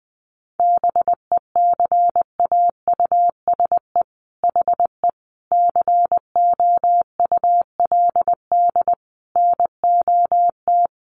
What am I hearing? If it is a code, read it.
Morse code, heard as a keyed tone, about 20 wpm, 700 Hz. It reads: BECAUSE HE COULD NOT
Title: Sentences from Top 1400 Words 20wpm